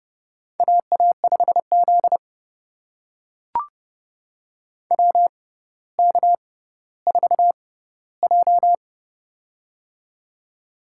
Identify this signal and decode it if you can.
Morse code, heard as a keyed tone, about 31 wpm, spaced as for 16 wpm, 700 Hz. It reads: AA5Z WK4J
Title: USA Call Signs - 4 Characters - 6x Character Spacing - 30wpm